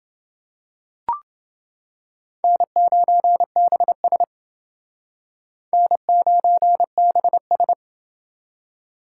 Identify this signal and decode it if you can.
Morse code, heard as a keyed tone, about 28 wpm, 700 Hz. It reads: N9BS N9BS